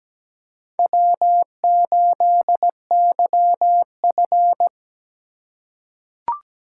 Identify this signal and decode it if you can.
Morse code, heard as a keyed tone, about 17 wpm, 700 Hz. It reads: W8YF
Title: USA Call Signs - 4 Characters - 2x Character Spacing - 17wpm